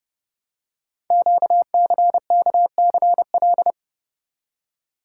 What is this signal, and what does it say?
Morse code, heard as a keyed tone, about 30 wpm, 700 Hz. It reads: QCKCL
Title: Code Groups - 5 Letters - 30wpm